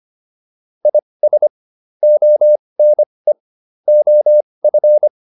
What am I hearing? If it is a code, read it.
Morse code, heard as a keyed tone, about 25 wpm, spaced as for 20 wpm, 600 Hz. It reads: IS ONE OF